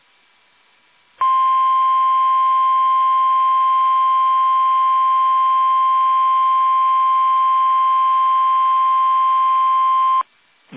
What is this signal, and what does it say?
Emergency Alert System (SAME) tones (the 1050 Hz attention tone). It is heard from 1.2 to 10.3 s.
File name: eas / KUSO / KUSO